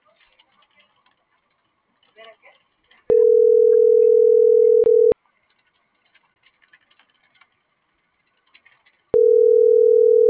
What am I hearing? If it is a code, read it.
Ringback tone (North America), 440 + 480 Hz.